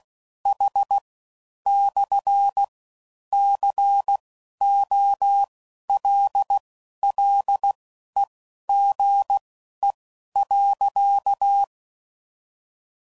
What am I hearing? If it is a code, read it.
Morse code, heard as a keyed tone, about 16 wpm, spaced as for 11 wpm, 800 Hz. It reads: H/COLLEGE.
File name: CW Code Groups @15WPM / QSOs / WD0BGP Spaced